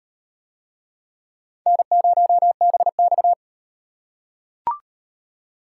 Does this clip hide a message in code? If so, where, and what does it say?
Yes: Morse code from 1.7 to 3.3 s, heard as a keyed tone, about 37 wpm, 700 Hz. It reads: N0BX